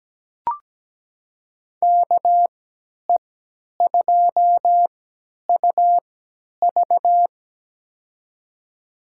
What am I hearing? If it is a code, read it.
Morse code, heard as a keyed tone, about 17 wpm, spaced as for 10 wpm, 700 Hz. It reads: KE2UV
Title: USA Call Signs - 5 Characters - 3x Character Spacing - 17wpm